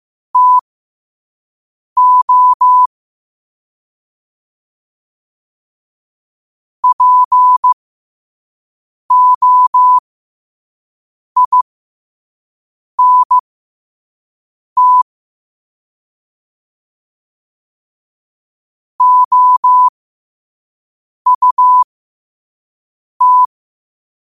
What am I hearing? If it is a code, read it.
Morse code, heard as a keyed tone, about 15 wpm, spaced as for 5 wpm, 1000 Hz. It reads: TO POINT OUT
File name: qotdfiles / 5wpm